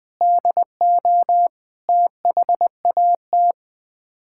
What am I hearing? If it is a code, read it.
Morse code, heard as a keyed tone, about 20 wpm, 700 Hz. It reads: DO THAT
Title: Sets of 8 Words - 20wpm